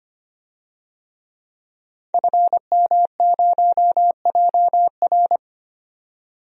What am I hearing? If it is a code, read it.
Morse code, heard as a keyed tone, about 25 wpm, 700 Hz. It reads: FM0JR
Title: DXCC-100 Call Signs - 5 Characters - 25wpm